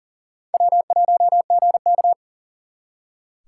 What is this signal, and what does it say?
Morse code, heard as a keyed tone, about 52 wpm, 700 Hz. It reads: W1GK